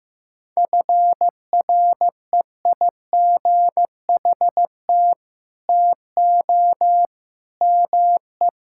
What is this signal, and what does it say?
Morse code, heard as a keyed tone, about 15 wpm, 700 Hz. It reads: FREIGHT TO ME
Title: Pangram Challenge v2 - 15wpm